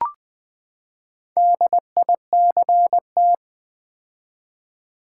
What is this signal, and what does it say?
Morse code, heard as a keyed tone, about 20 wpm, 700 Hz. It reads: DICT
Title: Build A Word - 4 Letters At A Time - 8-Letter Words - 20wpm